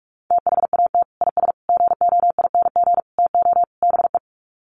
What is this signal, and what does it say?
Morse code, heard as a keyed tone, about 44 wpm, 700 Hz. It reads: THAT IS GOING TO BE
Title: Sets of 5 Words Encore2 45wpm